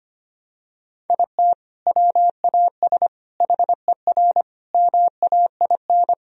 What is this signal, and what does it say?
Morse code, heard as a keyed tone, about 25 wpm, 700 Hz. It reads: IT WAS HER MAIN
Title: Sentences from Top 1000 Words - Review Part 2 - 25wpm